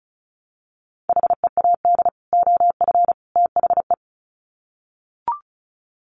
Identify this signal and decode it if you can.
Morse code, heard as a keyed tone, about 34 wpm, 700 Hz. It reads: HEAD OF THE
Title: Sets of 3 Words 35wpm